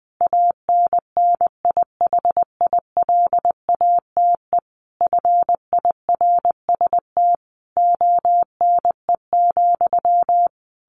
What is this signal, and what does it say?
Morse code, heard as a keyed tone, about 20 wpm, 700 Hz. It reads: ANNIHILATE FIRST ONE,
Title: A Sound Of Thunder 20wpm